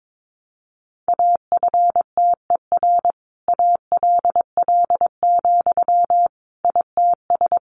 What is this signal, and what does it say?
Morse code, heard as a keyed tone, about 22 wpm, 700 Hz. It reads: AFTER ALL, ITS
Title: Sentences from Top 700 Words 22wpm